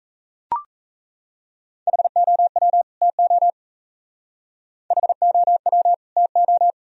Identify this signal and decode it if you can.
Morse code, heard as a keyed tone, about 36 wpm, 700 Hz. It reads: HOW TO HOW TO